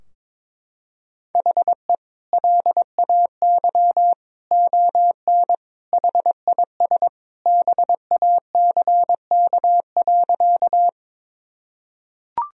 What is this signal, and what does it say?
Morse code, heard as a keyed tone, about 22 wpm, 700 Hz. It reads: HE LAY ON HIS BACK.
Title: Sentences from Top 1000 Words - Review Part 2 - 22wpm